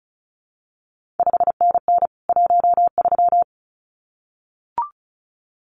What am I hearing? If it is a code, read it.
Morse code, heard as a keyed tone, about 36 wpm, 700 Hz. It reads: 5NN 13
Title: CQ WW DX Contest Exchange - 35wpm